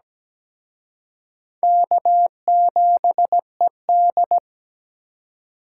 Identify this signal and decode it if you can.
Morse code, heard as a keyed tone, about 21 wpm, 700 Hz. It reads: K7ED
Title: Speed Racing - USA 1x2 Call Signs 17wpm